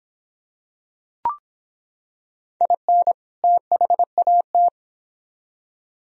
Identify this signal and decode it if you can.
Morse code, heard as a keyed tone, about 24 wpm, 700 Hz. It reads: IN THAT